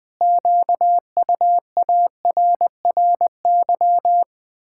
Morse code, heard as a keyed tone, about 20 wpm, 700 Hz. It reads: QUARRY
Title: Pangram Challenge v2 - 20wpm